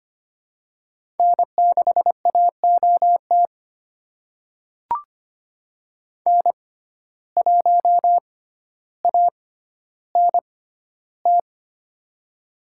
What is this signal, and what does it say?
Morse code, heard as a keyed tone, about 25 wpm, spaced as for 13 wpm, 700 Hz. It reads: N6AOT N1ANT
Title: USA Call Signs - 5 Characters - 6x Character Spacing - 25wpm